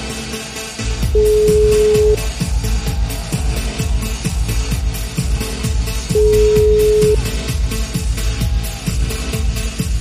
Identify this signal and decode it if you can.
Ringback tone (Europe, 425 Hz).